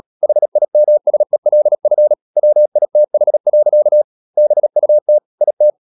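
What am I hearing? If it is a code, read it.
Morse code, heard as a keyed tone, about 37 wpm, 600 Hz. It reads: HIMSELF WITH. BUT IT
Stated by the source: Morse code practice at 35–39 wpm sending English text of Ken Follett's novel Never, from CW med Gnister.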